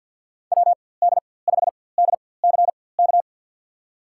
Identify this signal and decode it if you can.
Morse code, heard as a keyed tone, about 52 wpm, spaced as for 24 wpm, 700 Hz. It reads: WD5DCK
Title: USA Call Signs - 6 Characters - 4x Character Spacing - 50wpm